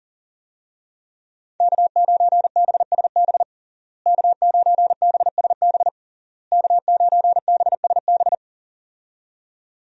Morse code, heard as a keyed tone, about 37 wpm, 700 Hz. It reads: K9BSB K9BSB K9BSB
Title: USA Call Signs - 5 Characters - Repeated 3x - 40wpm